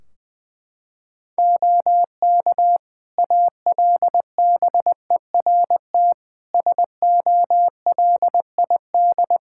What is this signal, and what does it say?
Morse code, heard as a keyed tone, about 20 wpm, 700 Hz. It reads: OK ALBERT SOLID
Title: Practice QSO - Exchange 3 - More Info - 20wpm